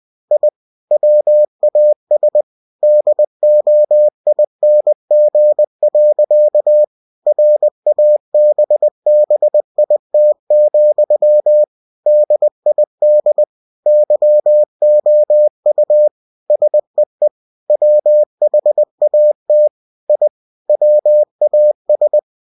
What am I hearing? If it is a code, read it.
Morse code, heard as a keyed tone, about 20 wpm, 600 Hz. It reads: I WAS DOING. RABBIT, DID YOU SEE WHAT I WAS
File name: cw-books / winnie-the-pooh / winnie-20wpm-page-0087